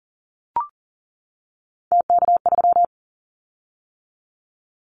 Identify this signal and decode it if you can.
Morse code, heard as a keyed tone, about 40 wpm, 700 Hz. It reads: TK3
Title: Callsign Prefixes for Top 100 DXCC - 40wpm